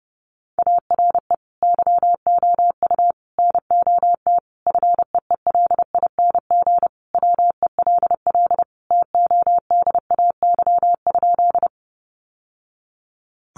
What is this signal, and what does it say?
Morse code, heard as a keyed tone, about 30 wpm, 700 Hz. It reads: ARE YOU NOT FEELING WELL TODAY?
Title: Sentences from Top 1200 Words 30wpm